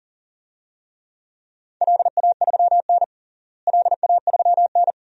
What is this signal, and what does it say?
Morse code, heard as a keyed tone, about 37 wpm, 700 Hz. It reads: LA3N LA3N
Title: DXCC-100 Call Signs - 4 Characters - Repeated 2x - 40wpm